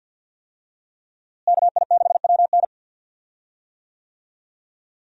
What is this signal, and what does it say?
Morse code, heard as a keyed tone, about 46 wpm, 700 Hz. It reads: KI6WN
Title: USA Call Signs - 5 Characters - Repeated 2x - 50wpm